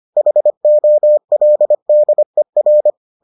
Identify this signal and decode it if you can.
Morse code, heard as a keyed tone, about 25 wpm, 600 Hz. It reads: HOLDER